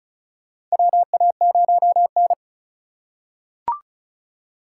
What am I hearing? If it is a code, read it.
Morse code, heard as a keyed tone, about 36 wpm, 700 Hz. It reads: WA0N
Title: USA Call Signs - 4 Characters - 2x Character Spacing - 35wpm